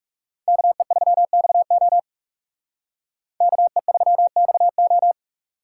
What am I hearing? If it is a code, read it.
Morse code, heard as a keyed tone, about 38 wpm, 700 Hz. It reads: KE3XO KE3XO